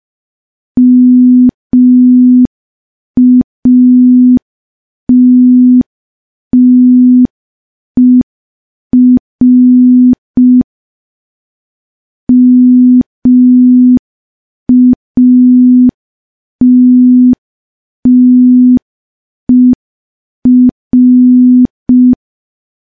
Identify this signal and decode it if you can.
Morse code, heard as a keyed tone, about 5 wpm, 250 Hz. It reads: MATTER MATTER
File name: morse / morse-training-05WPM